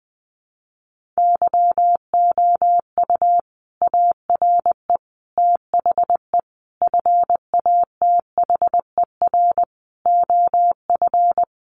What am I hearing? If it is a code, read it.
Morse code, heard as a keyed tone, about 20 wpm, 700 Hz. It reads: YOU ARE THE FATHER OF
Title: Sentences from Top 1500 Words 20wpm